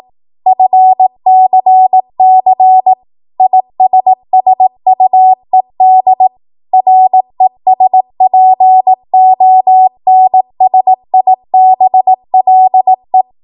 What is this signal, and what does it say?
Morse code, heard as a keyed tone, about 18 wpm, 750 Hz. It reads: FCC ISSUED RESPONSIBLE
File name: Morse / 18WPM